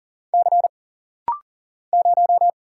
Morse code, heard as a keyed tone, about 40 wpm, 700 Hz. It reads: C 0
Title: Single Letter-Number Alt - Rapid-Fire 40wpm